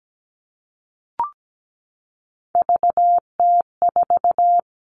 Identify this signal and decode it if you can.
Morse code, heard as a keyed tone, about 17 wpm, 700 Hz. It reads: VT4
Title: All Cut Numbers - 3-Digit 17wpm